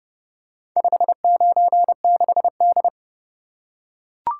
Morse code, heard as a keyed tone, about 29 wpm, 700 Hz. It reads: 596D